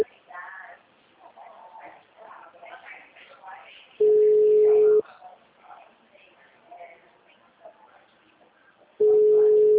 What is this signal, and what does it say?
Ringback tone (Europe, 425 Hz), with speech.